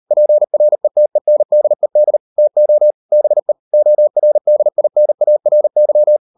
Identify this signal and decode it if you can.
Morse code, heard as a keyed tone, about 38 wpm, 600 Hz. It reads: PRETENDED TO BE ORDINARY